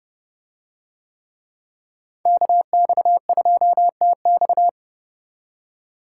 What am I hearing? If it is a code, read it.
Morse code, heard as a keyed tone, about 29 wpm, 700 Hz. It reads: KX2TX